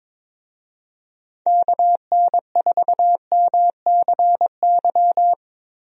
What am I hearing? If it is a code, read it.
Morse code, heard as a keyed tone, about 22 wpm, 700 Hz. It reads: KN4MCY
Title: USA Call Signs - 6 Characters - Repeated 2x - 22wpm